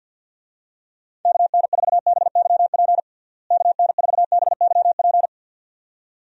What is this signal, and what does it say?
Morse code, heard as a keyed tone, about 50 wpm, 700 Hz. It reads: KN4BYP KN4BYP
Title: USA Call Signs - 6 Characters - Repeated 2x - 50wpm